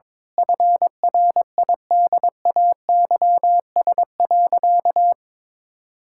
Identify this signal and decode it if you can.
Morse code, heard as a keyed tone, about 21 wpm, 700 Hz. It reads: FRIDAYS.
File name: Difficult Sentences with Hints - 22wpm